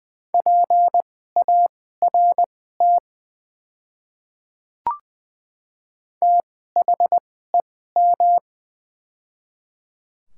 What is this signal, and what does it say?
Morse code, heard as a keyed tone, about 20 wpm, spaced as for 14 wpm, 700 Hz. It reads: PART THEM
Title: 2x Character Spacing - Top 200 Words - 20wpm